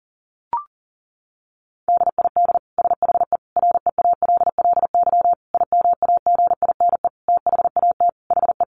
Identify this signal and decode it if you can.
Morse code, heard as a keyed tone, about 40 wpm, 700 Hz. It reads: DID SHE REALLY IMAGINE THAT HE
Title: Sentences from Top 1400 Words 40wpm